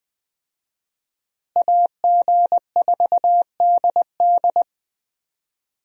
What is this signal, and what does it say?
Morse code, heard as a keyed tone, about 20 wpm, 700 Hz. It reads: AG4DD
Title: USA Call Signs - 5 Characters - Repeated 3x - 20wpm